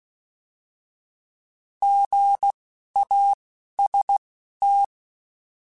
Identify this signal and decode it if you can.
Morse code, heard as a keyed tone, about 16 wpm, spaced as for 11 wpm, 800 Hz. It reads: GAST
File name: Alphabetical / Alphabetical mp3s / Gs @10